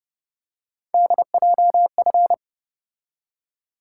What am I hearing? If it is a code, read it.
Morse code, heard as a keyed tone, about 30 wpm, 700 Hz. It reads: DJF